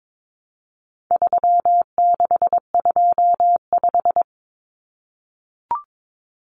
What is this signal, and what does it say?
Morse code, heard as a keyed tone, about 22 wpm, 700 Hz. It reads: 3625
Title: Common Cut Numbers - 4-Digit 22wpm